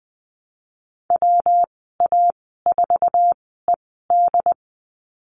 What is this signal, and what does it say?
Morse code, heard as a keyed tone, about 20 wpm, spaced as for 14 wpm, 700 Hz. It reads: WA4ED
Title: 122 Lesson 20g - Number 3 - Call Signs 20wpm